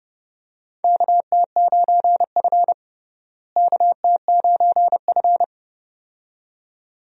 Morse code, heard as a keyed tone, about 32 wpm, 700 Hz. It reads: KT9F KT9F
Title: USA Call Signs - 4 Characters - Repeated 2x - 30wpm